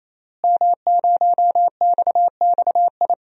Morse code, heard as a keyed tone, about 28 wpm, 700 Hz. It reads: M0XXI
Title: DXCC-100 Call Signs - 5 Characters - Repeated 3x - 28wpm